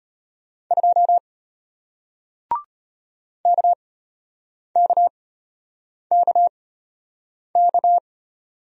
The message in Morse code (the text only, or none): 2 KKKK